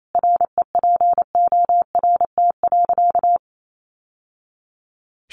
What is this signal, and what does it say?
Morse code, heard as a keyed tone, about 28 wpm, 690 Hz. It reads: REPORT.
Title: Sentences from Top 900 Words 28wpm